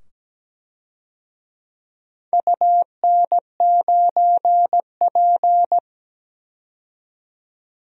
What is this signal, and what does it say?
Morse code, heard as a keyed tone, about 17 wpm, 700 Hz. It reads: UN9P
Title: DXCC-100 Call Signs - 4 Characters - 17wpm